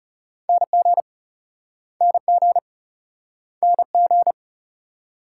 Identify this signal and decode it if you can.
Morse code, heard as a keyed tone, about 36 wpm, 700 Hz. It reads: NG NG NG
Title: Speed-Racing Multi-letter Phonemes - 30wpm